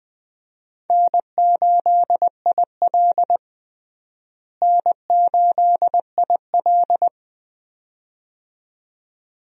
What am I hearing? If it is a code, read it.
Morse code, heard as a keyed tone, about 21 wpm, 700 Hz. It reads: N8IL N8IL
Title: USA Call Signs - 4 Characters - Repeated 2x - 20wpm